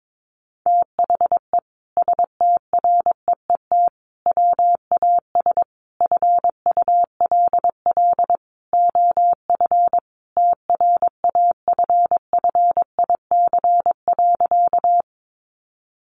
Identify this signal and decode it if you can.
Morse code, heard as a keyed tone, about 22 wpm, 700 Hz. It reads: THE STREET WAS FULL OF TRAFFIC.
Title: Sentences from Top 1300 Words 22wpm